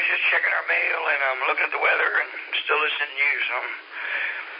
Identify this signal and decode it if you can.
Emergency Alert System (SAME) tones, heard as harsh buzzing bursts, with speech. It is heard from 0.4 to 4.0 s.